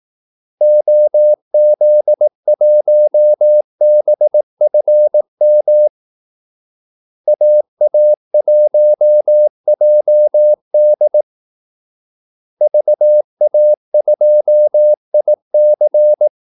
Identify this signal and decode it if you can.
Morse code, heard as a keyed tone, about 18 wpm, 600 Hz. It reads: OZ1BFM AA1JD VA2IC
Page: Kaldesignaler 18 wpm